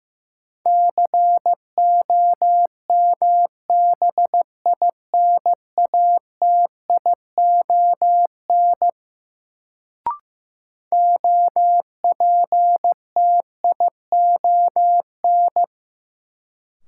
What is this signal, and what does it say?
Morse code, heard as a keyed tone, about 15 wpm, 700 Hz. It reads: COMBINATION OPTION
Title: Sentences from Top 1300 Words - New Word Drill Alt - 15wpm